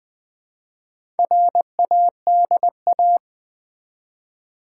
Morse code, heard as a keyed tone, about 20 wpm, 700 Hz. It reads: RADA
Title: Build A Word - 1 Letter At A Time - 5-Letter Words - 20wpm